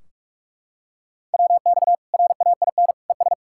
Morse code, heard as a keyed tone, about 46 wpm, 700 Hz. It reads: WX RAIN ES
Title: Practice QSO - Exchange 2 - Info - 45wpm